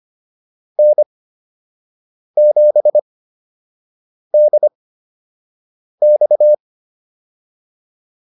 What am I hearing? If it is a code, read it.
Morse code, heard as a keyed tone, about 25 wpm, spaced as for 6 wpm, 600 Hz. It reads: N7DX
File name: QSO 102 06